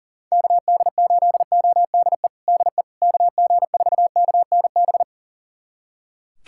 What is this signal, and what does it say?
Morse code, heard as a keyed tone, about 37 wpm, 700 Hz. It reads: KD8ODE DE KG4KNB
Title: USA Callsign DE Callsign - 6 Characters - 40wpm